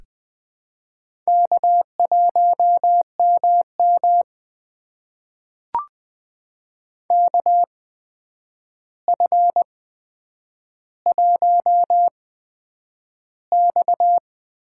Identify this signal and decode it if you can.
Morse code, heard as a keyed tone, about 20 wpm, spaced as for 9 wpm, 700 Hz. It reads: K1MM KF1X
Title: USA Call Signs - 4 Characters - 8x Character Spacing - 20wpm